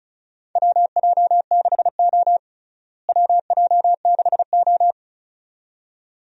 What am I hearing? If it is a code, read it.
Morse code, heard as a keyed tone, about 34 wpm, 700 Hz. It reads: WJ6O WJ6O
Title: USA Call Signs - 4 Characters - Repeated 2x - 35wpm